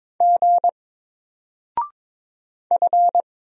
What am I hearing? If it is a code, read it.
Morse code, heard as a keyed tone, about 22 wpm, 700 Hz. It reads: G F
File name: Single-Letter-Number-22wpm